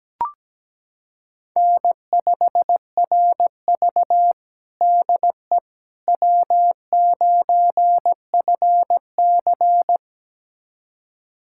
Morse code, heard as a keyed tone, about 17 wpm, 700 Hz. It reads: N5RV DE W9FC
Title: USA Callsign DE Callsign - 4 Characters - 17wpm